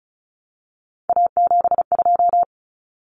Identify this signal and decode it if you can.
Morse code, heard as a keyed tone, about 34 wpm, 700 Hz. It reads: A72
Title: Common Cut Numbers - 3-Digit 35wpm